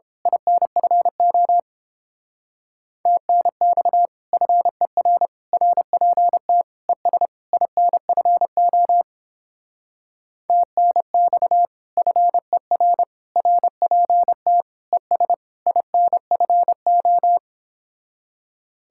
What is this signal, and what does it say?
Morse code, heard as a keyed tone, about 29 wpm, 700 Hz. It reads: INFO TNX FER RPT ES INFO TNX FER RPT ES INFO